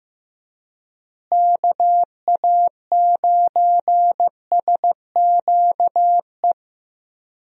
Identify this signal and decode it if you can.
Morse code, heard as a keyed tone, about 15 wpm, 700 Hz. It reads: KA9SQE
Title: USA Call Signs - 6 Characters - Repeated 2x - 15wpm